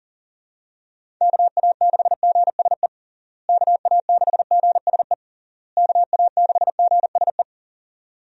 Morse code, heard as a keyed tone, about 40 wpm, 700 Hz. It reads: KA6GSE KA6GSE KA6GSE
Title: USA Call Signs - 6 Characters - Repeated 3x - 40wpm